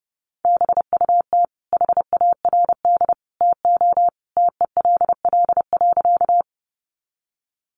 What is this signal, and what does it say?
Morse code, heard as a keyed tone, about 30 wpm, 700 Hz. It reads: BUT HARD TO TELL.
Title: Sentences from Top 300 Words 30wpm